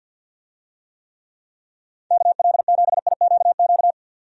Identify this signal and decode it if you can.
Morse code, heard as a keyed tone, about 48 wpm, 700 Hz. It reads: KL7IQQ
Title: DXCC-100 Call Signs - 6 Characters - 50wpm